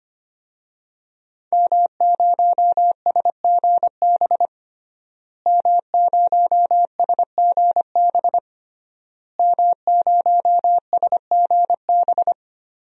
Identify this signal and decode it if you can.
Morse code, heard as a keyed tone, about 25 wpm, 700 Hz. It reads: M0SGB M0SGB M0SGB
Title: DXCC-100 Call Signs - 5 Characters - Repeated 3x - 25wpm